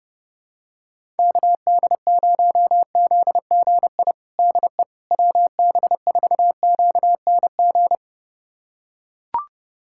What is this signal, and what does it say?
Morse code, heard as a keyed tone, about 32 wpm, 700 Hz. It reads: KD0ZGI DE WB4QNG
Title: USA Callsign DE Callsign - 6 Characters - 30wpm